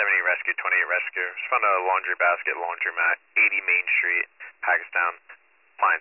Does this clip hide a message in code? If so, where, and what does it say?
Yes: Emergency Alert System (SAME) tones (header bursts) from 0.9 to 5.1 s, heard as harsh buzzing bursts, with speech.